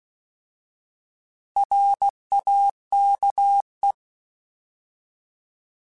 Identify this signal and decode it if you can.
Morse code, heard as a keyed tone, about 16 wpm, 800 Hz. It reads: RAKE
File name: Alphabetical / Alphabetical mp3s / Rs @15